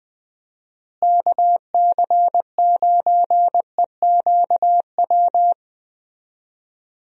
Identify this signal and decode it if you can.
Morse code, heard as a keyed tone, about 20 wpm, 700 Hz. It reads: KC9EQW
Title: USA Call Signs - 6 Characters - Repeated 2x - 20wpm